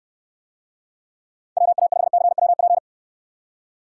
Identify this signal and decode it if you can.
Morse code, heard as a keyed tone, about 64 wpm, 700 Hz. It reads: WA5CWP